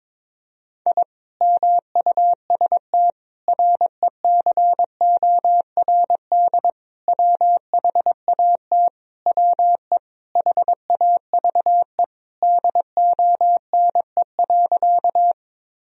Morse code, heard as a keyed tone, about 22 wpm, 700 Hz. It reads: I MUST RECORD WHAT WE HAVE DONE.
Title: Sentences from Top 1000 Words - Review Part 3 - 22wpm